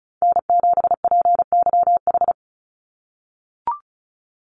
Morse code, heard as a keyed tone, about 34 wpm, 700 Hz. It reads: N7PYH
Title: USA Call Signs 35wpm